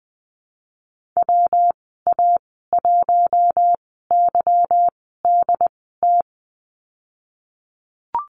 Morse code, heard as a keyed tone, about 19 wpm, spaced as for 14 wpm, 700 Hz. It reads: WA1YDT
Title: 193 Lesson 31g - Letter v - Call Signs 20wpm